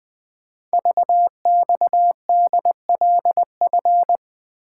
Morse code, heard as a keyed tone, about 20 wpm, 700 Hz. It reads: VXDLF
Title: Code Groups - 5 Letters - 20wpm